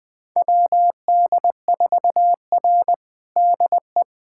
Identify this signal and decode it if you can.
Morse code, heard as a keyed tone, about 20 wpm, 700 Hz. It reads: WD4R DE
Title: USA Callsign DE Callsign - 4 Characters - 20wpm